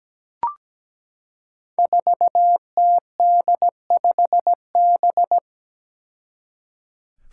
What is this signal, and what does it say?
Morse code, heard as a keyed tone, about 17 wpm, 700 Hz. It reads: 4TD5B
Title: Five Challenging Letter-Number Combinations 17wpm